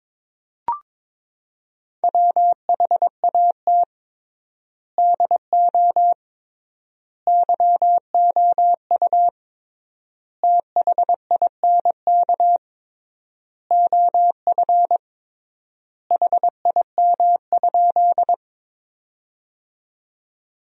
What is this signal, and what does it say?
Morse code, heard as a keyed tone, about 21 wpm, 700 Hz. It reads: WHAT DO YOU THINK OF HIM?
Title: 3x Word-Spacing - Sentences from Top 200 Words - 22wpm